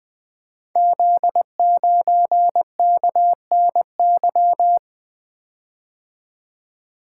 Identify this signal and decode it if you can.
Morse code, heard as a keyed tone, about 20 wpm, 700 Hz. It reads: Z9KNY
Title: Code Groups - 5 Letter:Numbers - 20wpm